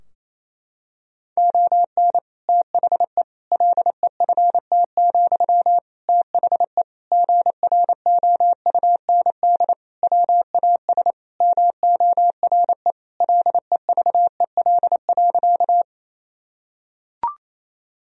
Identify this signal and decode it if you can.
Morse code, heard as a keyed tone, about 28 wpm, 700 Hz. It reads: ON THE LEFT, THE GROUND WAS MORE LEVEL.
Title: Sentences from Top 1000 Words - Review Part 4 - 28wpm